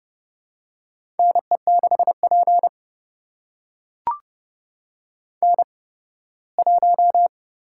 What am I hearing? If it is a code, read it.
Morse code, heard as a keyed tone, about 31 wpm, spaced as for 18 wpm, 700 Hz. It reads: NE6P N1
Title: USA Call Signs - 8x Character Spacing - 30wpm